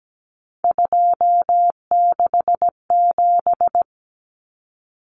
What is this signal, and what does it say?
Morse code, heard as a keyed tone, about 17 wpm, 700 Hz. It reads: 267
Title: Numbers - 3-Digit 17wpm